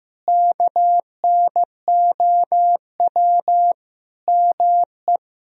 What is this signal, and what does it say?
Morse code, heard as a keyed tone, about 16 wpm, 700 Hz. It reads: KNOW ME
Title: Sentences from Top 400 Words and Abbreviations 15wpm